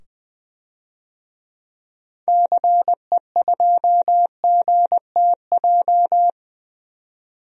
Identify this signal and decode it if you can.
Morse code, heard as a keyed tone, about 20 wpm, 700 Hz. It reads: CE2GTJ
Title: DXCC-100 Call Signs - 6 Characters - 20wpm